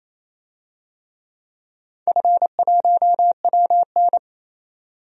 Morse code, heard as a keyed tone, about 28 wpm, 700 Hz. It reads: F1WN
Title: DXCC-100 Call Signs - 4 Characters - 28wpm